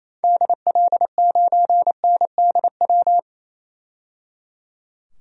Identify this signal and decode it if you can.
Morse code, heard as a keyed tone, about 28 wpm, 700 Hz. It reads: DL9NDW
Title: DXCC-100 Call Signs - 6 Characters - 28wpm